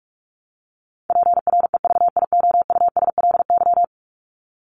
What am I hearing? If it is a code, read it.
Morse code, heard as a keyed tone, about 44 wpm, 700 Hz. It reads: PREVIOUSLY
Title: LY Suffix - Top 100 Words with Base Words 45wpm